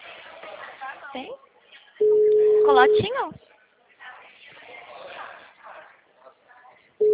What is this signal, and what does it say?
Ringback tone (Europe, 425 Hz), with speech.